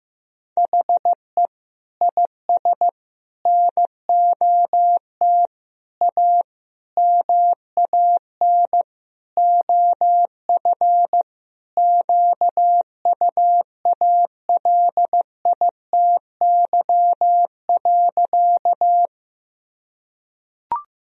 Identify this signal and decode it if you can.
Morse code, heard as a keyed tone, about 16 wpm, 700 Hz. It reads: HE IS NOT A MAN OF QUALITY.
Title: Sentences from Top 1000 Words - Review Part 4 - 15wpm